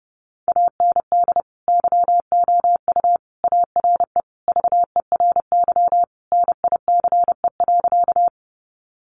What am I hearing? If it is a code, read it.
Morse code, heard as a keyed tone, about 30 wpm, 700 Hz. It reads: AND YOU ARE VERY NICE.
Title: Sentences from Top 1100 Words 30wpm